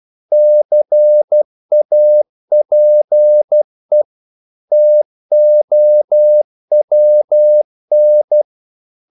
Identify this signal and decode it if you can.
Morse code, heard as a keyed tone, about 12 wpm, 600 Hz. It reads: CAPE TOWN
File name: CW / Practice / NationalCapitals12